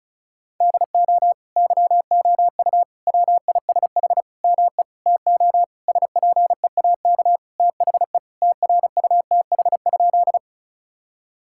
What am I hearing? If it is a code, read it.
Morse code, heard as a keyed tone, about 36 wpm, 700 Hz. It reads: DO YOU WISH ME TO SPEAK THE TRUTH?
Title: Sentences from Top 1600 Words 35wpm